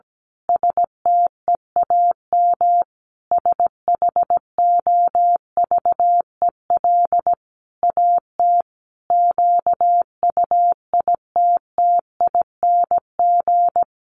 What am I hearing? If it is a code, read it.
Morse code, heard as a keyed tone, about 17 wpm, 700 Hz. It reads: STEAM SHOVEL AT QUITTING
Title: A Sound Of Thunder 17wpm